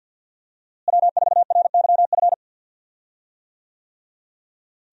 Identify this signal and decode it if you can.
Morse code, heard as a keyed tone, about 52 wpm, 700 Hz. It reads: W4RYF
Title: USA Call Signs - 5 Characters - Repeated 2x - 50wpm